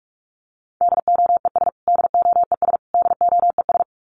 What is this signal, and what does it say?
Morse code, heard as a keyed tone, about 44 wpm, 700 Hz. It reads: DOES DOES DOES
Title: Top 500 Words Repeated 45wpm